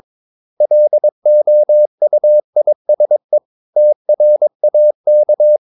Morse code, heard as a keyed tone, about 22 wpm, 600 Hz. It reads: LOUISE TRAK
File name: Ferie QRS 22wpm